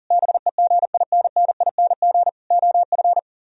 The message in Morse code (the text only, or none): BEGINNING OF